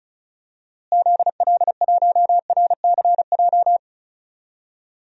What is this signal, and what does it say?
Morse code, heard as a keyed tone, about 34 wpm, 700 Hz. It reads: ZL1RCJ
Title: DXCC-100 Call Signs - 6 Characters - Repeated 2x - 35wpm